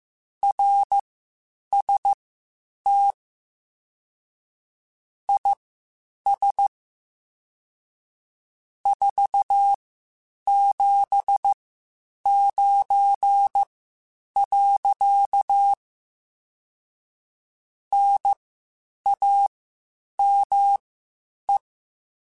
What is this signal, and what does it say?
Morse code, heard as a keyed tone, about 15 wpm, spaced as for 8 wpm, 800 Hz. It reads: RST IS 479. NAME